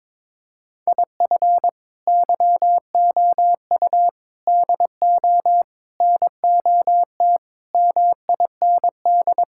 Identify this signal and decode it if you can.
Morse code, heard as a keyed tone, about 22 wpm, 700 Hz. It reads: IF YOU DO NOT MIND
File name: practice / 22wpm / Sentences with 4-Word Phrases - 22wpm